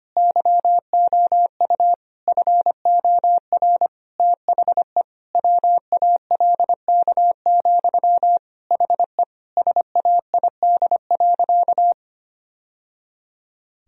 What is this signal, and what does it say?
Morse code, heard as a keyed tone, about 25 wpm, 700 Hz. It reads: YOU FOR THE WALK, HE SAID.
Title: Sentences from Top 1000 Words - Review Part 1 - 25wpm